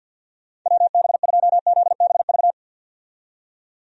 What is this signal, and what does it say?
Morse code, heard as a keyed tone, about 51 wpm, 700 Hz. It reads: WB2ZBV
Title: USA Call Signs - 6 Characters - 50wpm